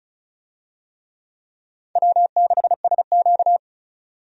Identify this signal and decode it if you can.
Morse code, heard as a keyed tone, about 34 wpm, 700 Hz. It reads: W6SQ
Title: USA Call Signs - 4 Characters - 35wpm